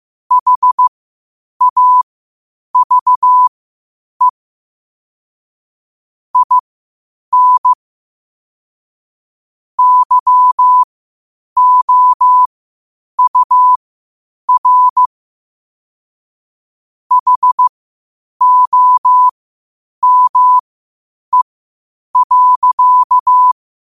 Morse code, heard as a keyed tone, about 15 wpm, spaced as for 8 wpm, 1000 Hz. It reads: HAVE IN YOUR HOME.
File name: qotdfiles / 8wpm